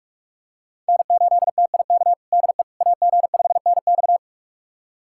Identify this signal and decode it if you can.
Morse code, heard as a keyed tone, about 42 wpm, 700 Hz. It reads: N8TIK DE AG5NX
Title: USA Callsign DE Callsign - 5 Characters - 45wpm